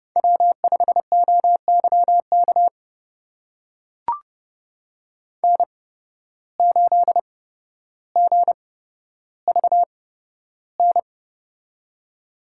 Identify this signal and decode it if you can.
Morse code, heard as a keyed tone, about 30 wpm, spaced as for 13 wpm, 700 Hz. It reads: W5OYK N8GVN
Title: USA Call Signs - 5 Characters - 8x Character Spacing - 30wpm